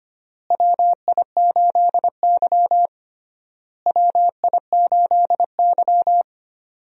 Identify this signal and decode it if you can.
Morse code, heard as a keyed tone, about 25 wpm, 700 Hz. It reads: WI8Y WI8Y
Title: USA Call Signs - 4 Characters - Repeated 3x - 25wpm